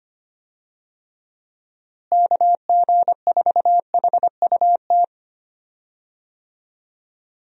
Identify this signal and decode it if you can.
Morse code, heard as a keyed tone, about 25 wpm, 700 Hz. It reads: KG4HUT
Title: USA Call Signs - 6 Characters - Repeated 3x - 25wpm